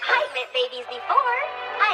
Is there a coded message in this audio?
No, only speech.